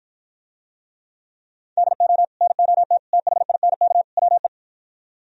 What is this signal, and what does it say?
Morse code, heard as a keyed tone, about 52 wpm, 700 Hz. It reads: DO NOT THINK WE